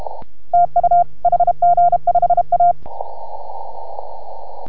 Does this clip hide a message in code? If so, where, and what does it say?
Yes: Morse code from 0.5 to 2.7 s, heard as a keyed tone, about 34 wpm, 690 Hz. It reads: TU HG5A